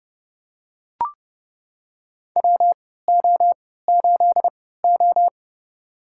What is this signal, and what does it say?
Morse code, heard as a keyed tone, about 30 wpm, spaced as for 17 wpm, 700 Hz. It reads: WO8O
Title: USA Call Signs - 4 Characters - 3x Character Spacing - 30wpm